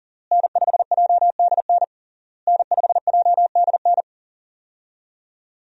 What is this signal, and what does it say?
Morse code, heard as a keyed tone, about 40 wpm, 700 Hz. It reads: N5JDN N5JDN
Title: USA Call Signs - 5 Characters - Repeated 3x - 40wpm